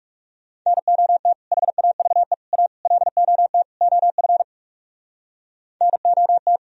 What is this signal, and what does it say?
Morse code, heard as a keyed tone, about 35 wpm, 700 Hz. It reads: NOT HAVE A LOT OF NOT
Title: Speed-Racing - Sets of 5 Words 30wpm